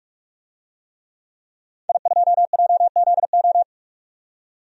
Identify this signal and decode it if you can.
Morse code, heard as a keyed tone, about 46 wpm, 700 Hz. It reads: I2JZO
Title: DXCC-100 Call Signs - 5 Characters - 45wpm